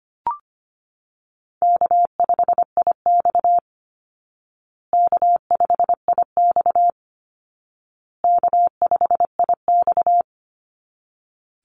Morse code, heard as a keyed tone, about 25 wpm, 700 Hz. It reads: K5IX K5IX K5IX
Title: Call Signs Repeated 2x - 25wpm